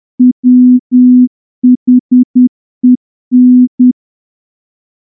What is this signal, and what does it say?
Morse code, heard as a keyed tone, about 9 wpm, 250 Hz. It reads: WHEN
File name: morse-training-10WPM